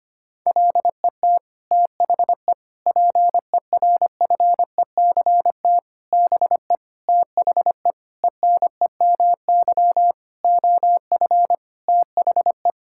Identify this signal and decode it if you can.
Morse code, heard as a keyed tone, about 25 wpm, 700 Hz. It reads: LET THE PERFECT BE THE ENEMY OF THE
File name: Sets of 9 Words - 25wpm